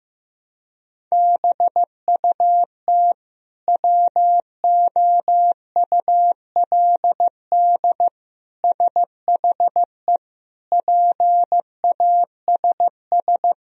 Morse code, heard as a keyed tone, about 16 wpm, 700 Hz. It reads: BUT WOULD SHE PASS
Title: Sentences from Top 1000 Words - Review Part 1 - 15wpm